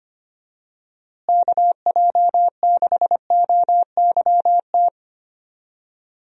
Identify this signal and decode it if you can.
Morse code, heard as a keyed tone, about 25 wpm, 700 Hz. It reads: KJ6OYT
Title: USA Call Signs - 6 Characters - Repeated 2x - 25wpm